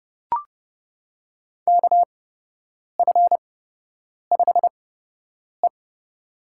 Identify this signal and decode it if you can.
Morse code, heard as a keyed tone, about 30 wpm, spaced as for 8 wpm, 700 Hz. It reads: KF5E
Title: USA Call Signs - 4 Characters - 8x Character Spacing - 30wpm